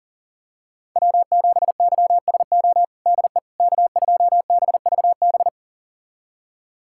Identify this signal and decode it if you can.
Morse code, heard as a keyed tone, about 40 wpm, 700 Hz. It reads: W7YSO DE K2BVB
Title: USA Callsign DE Callsign - 5 Characters - 40wpm